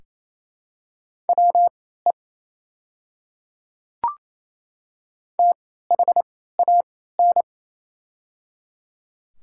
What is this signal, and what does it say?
Morse code, heard as a keyed tone, about 28 wpm, spaced as for 16 wpm, 700 Hz. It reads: WE THAN
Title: 3x Character Spacing - Top 100 Words - 28wpm